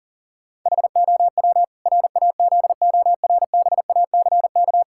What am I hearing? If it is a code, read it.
Morse code, heard as a keyed tone, about 40 wpm, 700 Hz. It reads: HOW RAZORBACK
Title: Pangram Challenge v2 - 40wpm